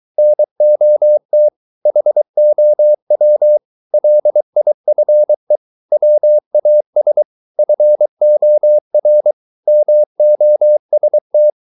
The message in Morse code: NOT HOW LIFE WAS FOR MOST

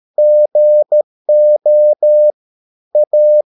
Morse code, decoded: GO A

